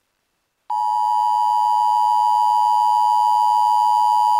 Emergency Alert System (SAME) tones (the 853 and 960 Hz attention tone). It is heard from 0.7 to 4.4 s.